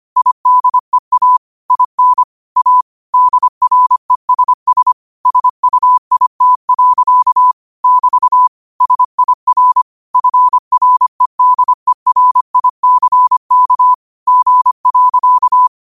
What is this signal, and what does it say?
Morse code, heard as a keyed tone, about 25 wpm, 1000 Hz. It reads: IDEA IN A DRESS SUIT. <BT> SIR FREDERICK G.
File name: qotdfiles / 25wpm